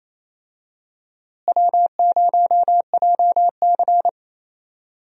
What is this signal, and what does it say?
Morse code, heard as a keyed tone, about 28 wpm, 700 Hz. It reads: W0JC
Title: USA Call Signs - 4 Characters - Repeated 3x - 28wpm